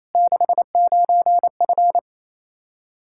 Morse code, heard as a keyed tone, about 28 wpm, 700 Hz. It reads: 69F
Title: K1USN SST QSO Practice 28wpm